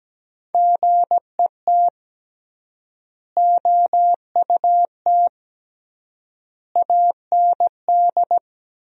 Morse code, heard as a keyed tone, about 17 wpm, 700 Hz. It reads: GET OUT AND
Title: 3x Word-Spacing - Sentences from Top 200 Words - 17wpm